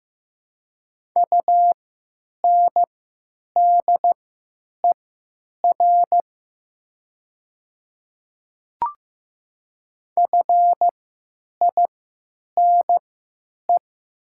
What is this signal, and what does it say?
Morse code, heard as a keyed tone, about 15 wpm, spaced as for 8 wpm, 700 Hz. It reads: UNDER FINE
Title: 3x Character Spacing - Top 500 Words - 15wpm